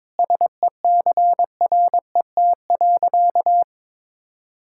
Morse code, heard as a keyed tone, about 23 wpm, 700 Hz. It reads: SECRET.
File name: practice / 22wpm / Sentences with 3-Word Phrases - 22wpm